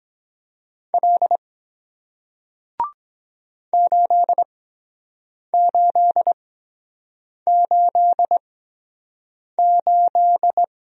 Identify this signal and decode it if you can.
Morse code, heard as a keyed tone, about 23 wpm, spaced as for 7 wpm, 700 Hz. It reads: L 8888